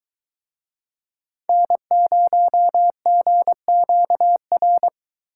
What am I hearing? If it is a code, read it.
Morse code, heard as a keyed tone, about 22 wpm, 700 Hz. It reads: N0GQR